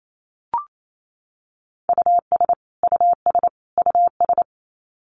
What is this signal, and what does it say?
Morse code, heard as a keyed tone, about 28 wpm, 700 Hz. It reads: US US US
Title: Top 200 Words Repeated 28wpm